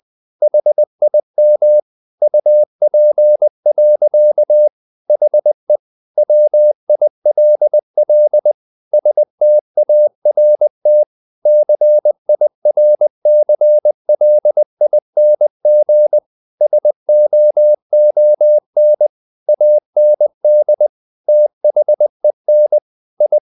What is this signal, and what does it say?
Morse code, heard as a keyed tone, about 20 wpm, 600 Hz. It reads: HIM UP. HE WILL START CIRCLING SOON AND THEN I
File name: the old man and the sea part 2 20wpm